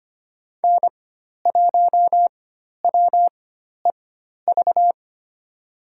Morse code, heard as a keyed tone, about 25 wpm, spaced as for 12 wpm, 700 Hz. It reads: N1WEV
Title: USA Call Signs - 5 Characters - 4x Character Spacing - 25wpm